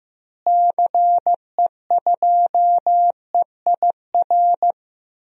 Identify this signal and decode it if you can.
Morse code, heard as a keyed tone, about 15 wpm, 700 Hz. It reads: CE2EIR
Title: DXCC-100 Call Signs - 6 Characters - Repeated 3x - 15wpm